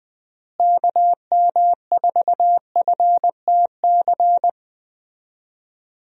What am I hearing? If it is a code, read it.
Morse code, heard as a keyed tone, about 20 wpm, 700 Hz. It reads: KM4FTC
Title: USA Call Signs - 6 Characters - Repeated 3x - 20wpm